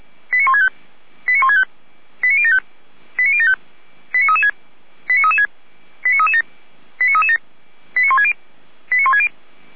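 Five-tone selective call (ZVEI), heard as quick runs of five tones, tones of about 70 ms. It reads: 89165 89165 89075 89075 89336 89336 89337 89337 89160 89160
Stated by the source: ZVEI five-tone selective call to address 89073, received via SDR by Feuerwehr Orlamünde.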